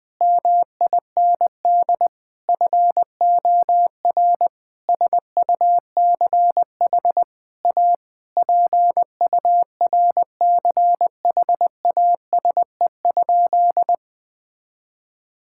Morse code, heard as a keyed tone, about 20 wpm, 700 Hz. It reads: MIND FOR SUCH A PURCHASE?
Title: Sentences from Top 1000 Words - Review Part 6 - 20wpm